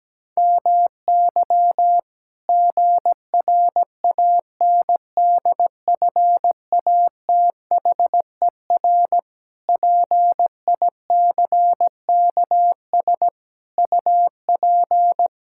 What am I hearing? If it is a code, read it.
Morse code, heard as a keyed tone, about 17 wpm, 700 Hz. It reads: MY GRANDFATHER PICKS UP
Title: Pangram Challenge v2 - 17wpm